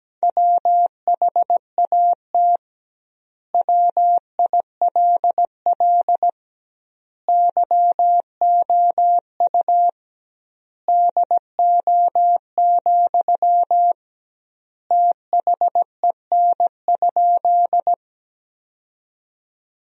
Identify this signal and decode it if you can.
Morse code, heard as a keyed tone, about 17 wpm, 700 Hz. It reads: WHAT WILL YOU DO, THEN?
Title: 2x Word-Spacing - Sentences from Top 100 Words - 17wpm